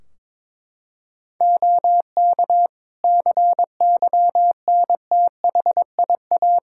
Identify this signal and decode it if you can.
Morse code, heard as a keyed tone, about 21 wpm, 700 Hz. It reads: OK CYNTHIA
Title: Practice QSO - Exchange 3 - More Info - 22wpm